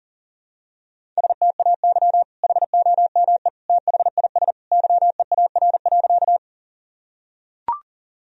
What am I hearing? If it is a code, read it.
Morse code, heard as a keyed tone, about 40 wpm, 700 Hz. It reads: STAY HOME THIS YEAR.
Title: Sentences from Top 1000 Words - Review Part 1 - 40wpm